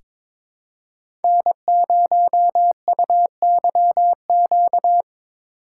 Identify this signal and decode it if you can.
Morse code, heard as a keyed tone, about 22 wpm, 700 Hz. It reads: N0UYQ
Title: USA Call Signs - 5 Characters - 2x Character Spacing - 22wpm